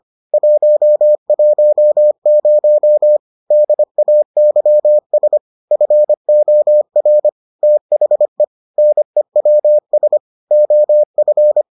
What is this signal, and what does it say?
Morse code, heard as a keyed tone, about 25 wpm, 600 Hz. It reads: 110 DAYS FOR THE NEWS OF